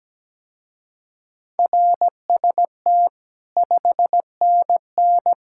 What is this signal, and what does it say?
Morse code, heard as a keyed tone, about 17 wpm, 700 Hz. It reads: RST 5NN
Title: Speed-Racing Common QSO Phrases - 17wpm